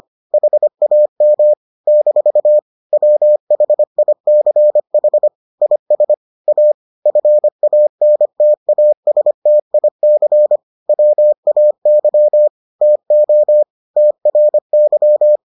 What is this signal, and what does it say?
Morse code, heard as a keyed tone, about 25 wpm, 600 Hz. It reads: HAM - WHICH IS A FANTASTIC WAY TO TRY